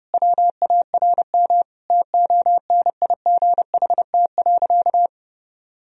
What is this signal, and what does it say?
Morse code, heard as a keyed tone, about 30 wpm, 700 Hz. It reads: WARM TONIGHT.
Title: Sentences from Top 1000 Words - Review Part 3 - 30wpm